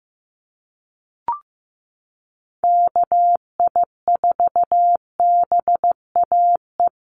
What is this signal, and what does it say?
Morse code, heard as a keyed tone, about 15 wpm, 700 Hz. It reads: KI4BAE
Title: USA Call Signs Repeated 2x 15wpm